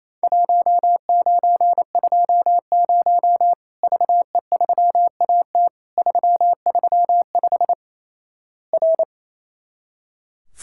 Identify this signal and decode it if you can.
Morse code, heard as a keyed tone, about 28 wpm, 700 Hz. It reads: 1920 VE3AT 335 R
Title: Worked All Europe DX Contest - QTC Exchange - 28wpm